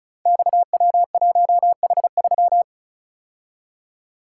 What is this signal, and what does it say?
Morse code, heard as a keyed tone, about 36 wpm, 700 Hz. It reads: XW1H3
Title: Code Groups - 5 Letter:Numbers - 35wpm